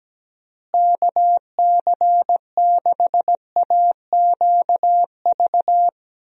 Morse code, heard as a keyed tone, about 17 wpm, 700 Hz. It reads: KC6AQV
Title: USA Call Signs - 6 Characters - Repeated 2x - 17wpm